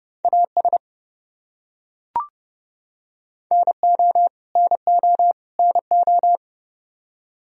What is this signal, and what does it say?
Morse code, heard as a keyed tone, about 31 wpm, 700 Hz. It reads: AS NO NO NO
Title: Top 100 Words Repeated 30wpm